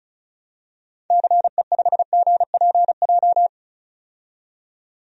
Morse code, heard as a keyed tone, about 36 wpm, 700 Hz. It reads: CE5GPJ
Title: DXCC-100 Call Signs - 6 Characters - 35wpm